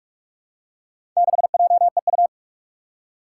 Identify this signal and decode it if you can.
Morse code, heard as a keyed tone, about 44 wpm, 700 Hz. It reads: 6JEU
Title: Four Challenging Letter-Number Combinations 45wpm